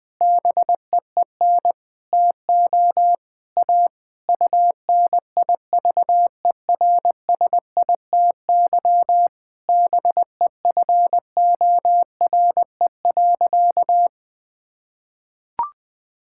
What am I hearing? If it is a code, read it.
Morse code, heard as a keyed tone, about 21 wpm, 700 Hz. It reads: BEEN TO A UNIVERSITY BEFORE.
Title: Sentences from Top 1000 Words - Review Part 5 - 20wpm